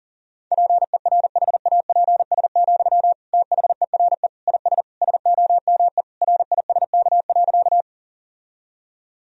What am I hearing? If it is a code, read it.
Morse code, heard as a keyed tone, about 40 wpm, 700 Hz. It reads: PERHAPS, THERE IS SOME RISK.
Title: Sentences from Top 1000 Words - Review Part 4 - 40wpm